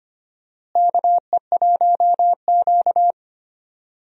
Morse code, heard as a keyed tone, about 25 wpm, 700 Hz. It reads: KE1Q
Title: USA Call Signs - 4 Characters - Repeated 2x - 25wpm